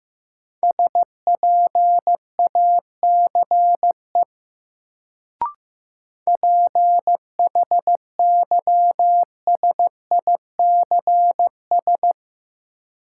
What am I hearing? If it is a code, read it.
Morse code, heard as a keyed tone, about 15 wpm, 700 Hz. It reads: SPACE PHYSICS
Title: Sentences from Top 1000 Words - Review Part 5 - New Word Drill - 15wpm